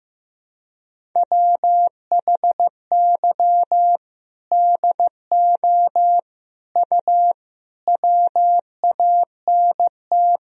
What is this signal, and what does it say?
Morse code, heard as a keyed tone, about 15 wpm, 700 Hz. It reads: WHY DO U WANT
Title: Sentences from Top 200 Words and Abbreviations 15wpm